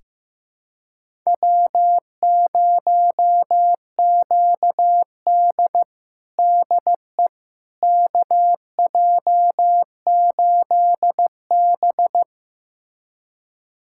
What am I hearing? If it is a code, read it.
Morse code, heard as a keyed tone, about 15 wpm, 700 Hz. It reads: W0QD DE KJ8B
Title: USA Callsign DE Callsign - 4 Characters - 15wpm